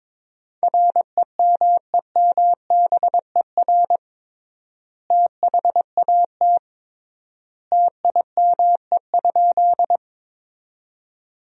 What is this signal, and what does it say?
Morse code, heard as a keyed tone, about 22 wpm, 700 Hz. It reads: REMEMBER THAT TIME?
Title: 3x Word-Spacing - Sentences from Top 400 Words - 22wpm